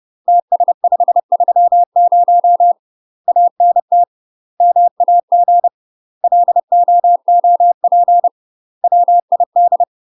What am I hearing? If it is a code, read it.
Morse code, heard as a keyed tone, about 30 wpm, 700 Hz. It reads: TS530 ANT MAG LOOP WID
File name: UK MASTER QSOs TRAINING / 30WPM / UK MASTER QSOs 0041